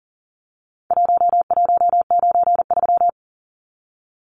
Morse code, heard as a keyed tone, about 40 wpm, 700 Hz. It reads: 1193
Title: Numbers - Contest Style - 4-Digit 40wpm